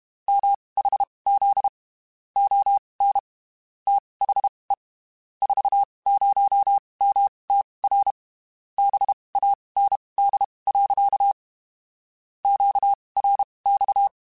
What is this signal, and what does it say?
Morse code, heard as a keyed tone, about 31 wpm, spaced as for 23 wpm, 800 Hz. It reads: MHZ ON THE 40MTR BAND. QRX
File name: (spaced) / QSOs Spaced-mp3s / kc7sac @30 sp